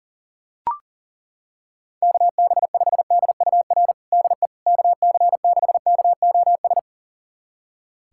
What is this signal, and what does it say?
Morse code, heard as a keyed tone, about 40 wpm, 700 Hz. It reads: KB5DUR DE KC6KOS
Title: USA Callsign DE Callsign - 6 Characters - 40wpm